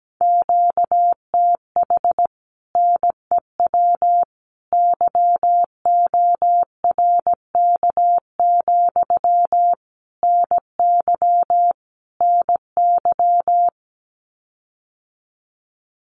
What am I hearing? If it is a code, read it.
Morse code, heard as a keyed tone, about 17 wpm, 700 Hz. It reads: QTH NEW YORK, NY NY
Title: Common QSO Phrases Encore 17wpm